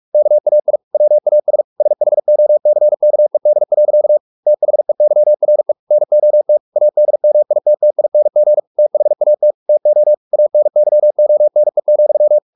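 Morse code, heard as a keyed tone, about 44 wpm, 600 Hz. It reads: KAI WAS SHOCKED. THEYRE NOT ADMITTING THAT TO ANYONE,
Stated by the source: Morse practice at 45 wpm, the novel Never by Ken Follett in English, from CW med Gnister.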